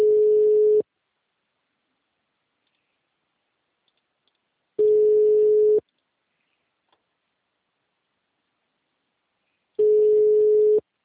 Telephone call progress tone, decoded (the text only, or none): ringback tone